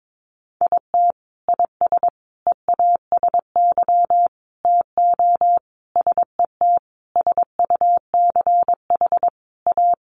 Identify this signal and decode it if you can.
Morse code, heard as a keyed tone, about 22 wpm, 700 Hz. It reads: IT IS EASY TO SET SUCH A
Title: Sentences from Top 1300 Words 22wpm